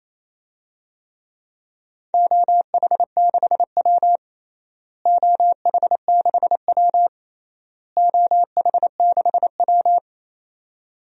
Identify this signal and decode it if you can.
Morse code, heard as a keyed tone, about 28 wpm, 700 Hz. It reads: OH6W OH6W OH6W
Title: DXCC-100 Call Signs - 4 Characters - Repeated 3x - 28wpm